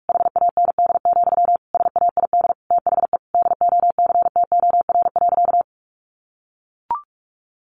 Morse code, heard as a keyed tone, about 44 wpm, 700 Hz. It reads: HAND, SAID THE DOCTOR.
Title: Sentences from Top 1500 Words 45wpm